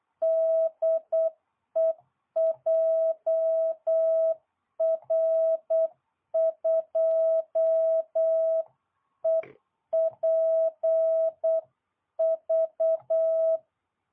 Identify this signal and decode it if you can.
Morse code, heard as a keyed tone, about 8 wpm, 660 Hz. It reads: DE JR2EPV